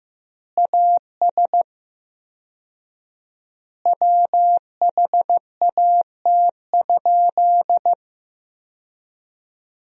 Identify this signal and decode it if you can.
Morse code, heard as a keyed tone, about 15 wpm, 700 Hz. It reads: AS WHAT?
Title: 4x Word-Spacing - Sentences from Top 200 Words - 15wpm